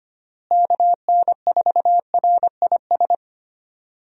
Morse code, heard as a keyed tone, about 25 wpm, 700 Hz. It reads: KN4RIS
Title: USA Call Signs - 6 Characters - Repeated 3x - 25wpm